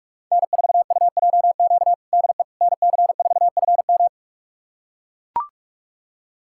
Morse code, heard as a keyed tone, about 42 wpm, 700 Hz. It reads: N4UJQ DE NC4FM
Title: USA Callsign DE Callsign - 5 Characters - 45wpm